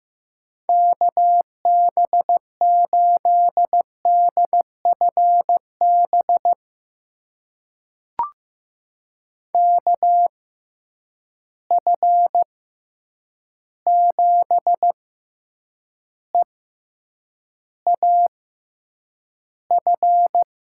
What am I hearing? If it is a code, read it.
Morse code, heard as a keyed tone, about 15 wpm, spaced as for 8 wpm, 700 Hz. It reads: KB8DFB KF7EAF
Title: USA Call Signs - 6 Characters - 6x Character Spacing - 15wpm